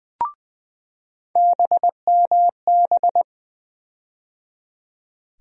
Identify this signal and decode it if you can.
Morse code, heard as a keyed tone, about 20 wpm, 700 Hz. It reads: BMB
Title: USA Call Sign Postfixes - 3 Letters 20wpm